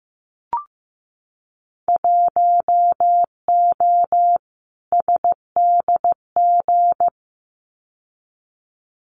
Morse code, heard as a keyed tone, about 15 wpm, 700 Hz. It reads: 1O SDG
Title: Winter Field Day Exchanges Encore 15wpm